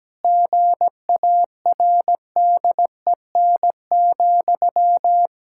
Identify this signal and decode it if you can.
Morse code, heard as a keyed tone, about 17 wpm, 700 Hz. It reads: GARDEN,
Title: Pangram Challenge v2 - 17wpm